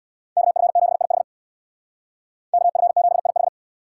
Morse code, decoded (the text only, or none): KF7IH KF7IH